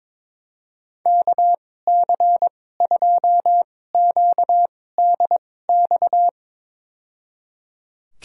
Morse code, heard as a keyed tone, about 22 wpm, spaced as for 16 wpm, 700 Hz. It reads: KC2QDX
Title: USA Call Signs - 6 Characters - 2x Character Spacing - 22wpm